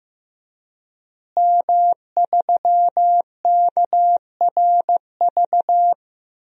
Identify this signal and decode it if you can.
Morse code, heard as a keyed tone, about 15 wpm, 700 Hz. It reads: M3KRV
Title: DXCC-100 Call Signs - 5 Characters - Repeated 2x - 15wpm